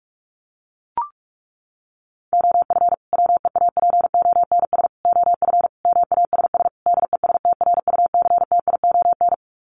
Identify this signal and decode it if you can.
Morse code, heard as a keyed tone, about 44 wpm, 700 Hz. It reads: OF WEAPONS OF MASS DESTRUCTION
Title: Sets of 5 Words Encore 45wpm